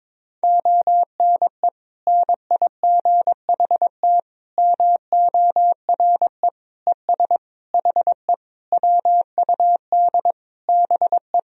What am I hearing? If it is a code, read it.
Morse code, heard as a keyed tone, about 22 wpm, 700 Hz. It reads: ONE NIGHT MORE ES HE WUD BE
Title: Sentences from Top 300 Words and Abbreviations 22wpm